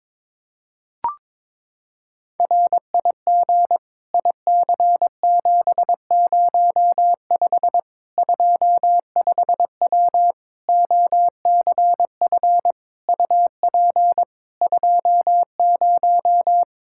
Morse code, heard as a keyed tone, about 22 wpm, 700 Hz. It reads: RIG IC705 25W OCF UP 20
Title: K1USN SST QSO Practice 22wpm